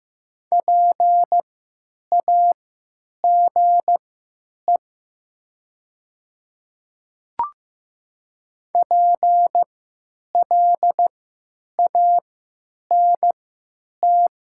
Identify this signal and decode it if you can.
Morse code, heard as a keyed tone, about 15 wpm, spaced as for 9 wpm, 700 Hz. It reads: PAGE PLANT
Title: 3x Character Spacing - Top 500 Words - 15wpm